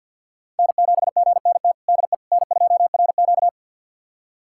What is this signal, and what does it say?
Morse code, heard as a keyed tone, about 52 wpm, 700 Hz. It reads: N7GNT DE N2RQ
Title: USA Callsign DE Callsign 50wpm